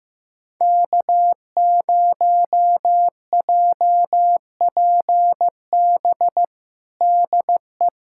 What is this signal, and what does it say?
Morse code, heard as a keyed tone, about 16 wpm, 700 Hz. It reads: K0JPB DE
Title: USA Callsign DE Callsign - 5 Characters - 15wpm